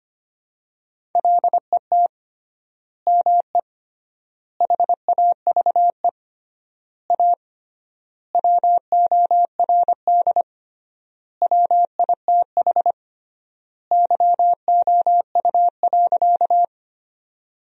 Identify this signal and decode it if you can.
Morse code, heard as a keyed tone, about 25 wpm, 700 Hz. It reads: LET ME HAVE A WORD WITH YOU.
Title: 3x Word-Spacing - Sentences from Top 300 Words - 25wpm